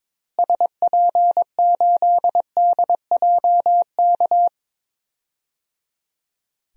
Morse code, heard as a keyed tone, about 22 wpm, 700 Hz. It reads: SP8DJK
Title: DXCC-100 Call Signs - 6 Characters - Repeated 3x - 22wpm